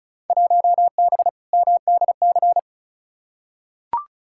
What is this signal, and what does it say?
Morse code, heard as a keyed tone, about 34 wpm, 700 Hz. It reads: 1B MDC
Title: Field Day Exchanges - 35wpm